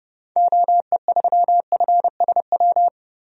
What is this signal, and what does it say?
Morse code, heard as a keyed tone, about 30 wpm, 700 Hz. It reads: OE3FSW
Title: DXCC-100 Call Signs - 6 Characters - Repeated 3x - 30wpm